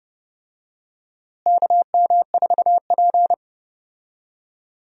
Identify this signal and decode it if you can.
Morse code, heard as a keyed tone, about 30 wpm, 700 Hz. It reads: KM4P
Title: USA Call Signs - 4 Characters - Repeated 3x - 30wpm